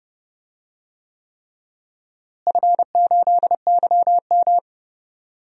Morse code, heard as a keyed tone, about 29 wpm, 700 Hz. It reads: F8YM
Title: DXCC-100 Call Signs - 4 Characters - Repeated 3x - 30wpm